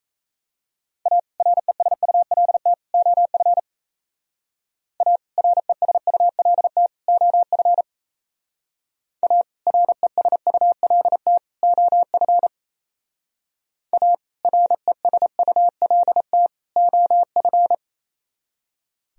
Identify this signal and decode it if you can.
Morse code, heard as a keyed tone, about 36 wpm, 700 Hz. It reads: A RESULT OF A RESULT OF A RESULT OF A RESULT OF